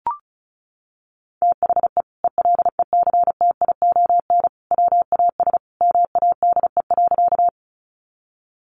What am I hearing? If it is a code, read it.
Morse code, heard as a keyed tone, about 34 wpm, 700 Hz. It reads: THE ELECTION WAS MADE.
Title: Sentences from Top 1500 Words 35wpm